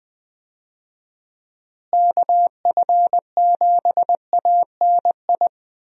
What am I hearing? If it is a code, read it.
Morse code, heard as a keyed tone, about 20 wpm, 700 Hz. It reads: KF7ANI
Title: USA Call Signs - 6 Characters - Repeated 2x - 20wpm